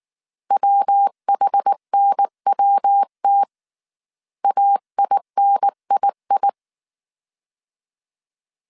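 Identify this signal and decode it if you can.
Morse code, heard as a keyed tone, about 19 wpm, 800 Hz. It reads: WHNWT AINII
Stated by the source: Morse code dictation at 18 wpm from the IW3SGT QRS Morse code training course, five-character groups of 300 characters.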